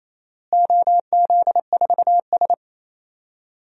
Morse code, heard as a keyed tone, about 28 wpm, 700 Hz. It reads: OZ4S
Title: DXCC-100 Call Signs - 4 Characters - Repeated 3x - 28wpm